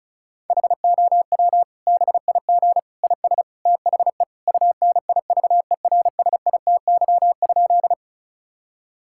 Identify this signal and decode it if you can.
Morse code, heard as a keyed tone, about 36 wpm, 700 Hz. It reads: HOW BIG IS THE UNIVERSITY?
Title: Sentences from Top 1000 Words - Review Part 5 - 35wpm